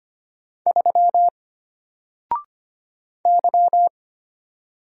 Morse code, heard as a keyed tone, about 25 wpm, 700 Hz. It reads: 3 Y
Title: Single Letter-Number Alt 25wpm